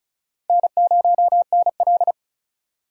Morse code, heard as a keyed tone, about 45 wpm, 700 Hz. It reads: N0NL